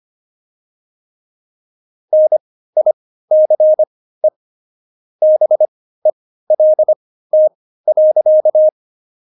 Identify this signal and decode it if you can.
Morse code, heard as a keyed tone, about 25 wpm, spaced as for 15 wpm, 620 Hz. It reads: NICE BELT.
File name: Jokes-25-char-15-eff